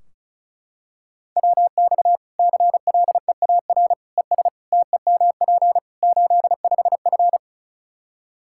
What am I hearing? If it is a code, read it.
Morse code, heard as a keyed tone, about 36 wpm, 700 Hz. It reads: WX CLEAR ES TEMP 85F
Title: Practice QSO - Exchange 2 - Info - 35wpm